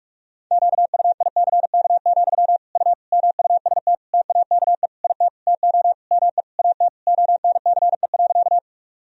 Morse code, heard as a keyed tone, about 42 wpm, 700 Hz. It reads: QUICK, U MUST TAKE IT TO ME AT ONCE.
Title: Sentences from Top 500 Words and Abbreviations 45wpm